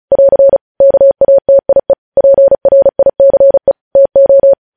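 Morse code, heard as a keyed tone, about 36 wpm, 550 Hz. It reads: <AR> KATIE PRICE TO
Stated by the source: Morse code practice sending news headlines at 35 wpm.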